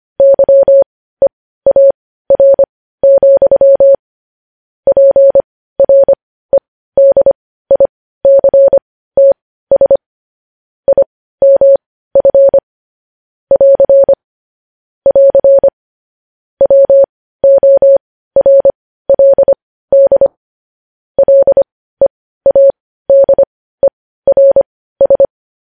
Morse code, heard as a keyed tone, about 25 wpm, spaced as for 15 wpm, 550 Hz. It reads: YEAR, PREDICTS IMF <AR> <AR> WORLD LEADERS
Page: News headlines in morse code at 15 wpm